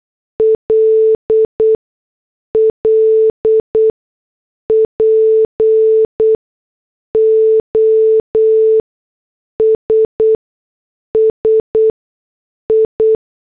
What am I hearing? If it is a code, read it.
Morse code, heard as a keyed tone, about 8 wpm, spaced as for 6 wpm, 440 Hz. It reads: LL POSSI
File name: morse / headlines